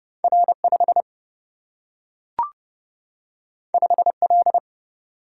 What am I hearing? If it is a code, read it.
Morse code, heard as a keyed tone, about 30 wpm, 700 Hz. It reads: R 5 5 L
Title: Single-Letter Single-Number 30wpm